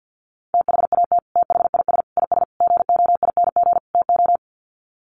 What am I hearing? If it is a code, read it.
Morse code, heard as a keyed tone, about 50 wpm, 700 Hz. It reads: THAT THIS IS GOING TO
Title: Sets of 5 Words Encore2 50wpm